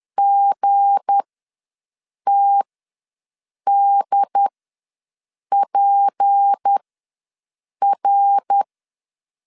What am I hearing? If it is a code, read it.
Morse code, heard as a keyed tone, about 10 wpm, spaced as for 6 wpm, 800 Hz. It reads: GTDPR
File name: LEZIONI / 7 GF9Z / 7 G ws 10wpm 3843